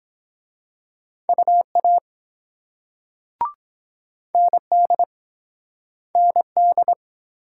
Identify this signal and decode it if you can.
Morse code, heard as a keyed tone, about 23 wpm, 700 Hz. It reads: UA ND ND